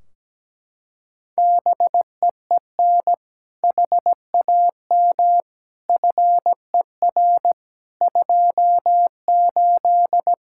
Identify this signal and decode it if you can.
Morse code, heard as a keyed tone, about 17 wpm, 700 Hz. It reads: BEEN HAM FER 28
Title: Practice QSO - Exchange 3 - More Info - 17wpm